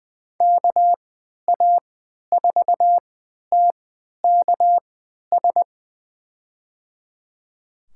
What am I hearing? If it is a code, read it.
Morse code, heard as a keyed tone, about 20 wpm, spaced as for 11 wpm, 700 Hz. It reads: KA4TKS
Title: USA Call Signs - 3x Character Spacing - 20wpm